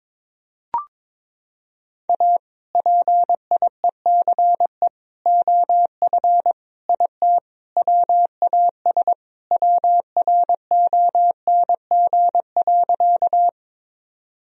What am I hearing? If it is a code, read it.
Morse code, heard as a keyed tone, about 22 wpm, 700 Hz. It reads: A PIECE OF IT WAS WRONG.
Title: Sentences from Top 1000 Words - Review Part 2 - 22wpm